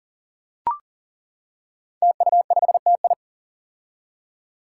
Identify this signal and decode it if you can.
Morse code, heard as a keyed tone, about 40 wpm, 700 Hz. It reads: TU5TI
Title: Five Challenging Letter-Number Combinations 40wpm